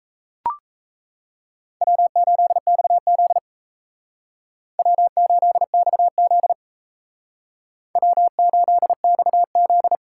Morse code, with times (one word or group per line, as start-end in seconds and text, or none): 1.8-3.4 W8XZ
4.8-6.5 W8XZ
8.0-9.9 W8XZ